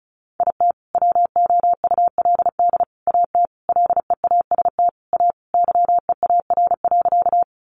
Morse code, heard as a keyed tone, about 34 wpm, 700 Hz. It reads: IT WOULD AT LEAST A YEAR.
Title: Sentences from Top 700 Words 35wpm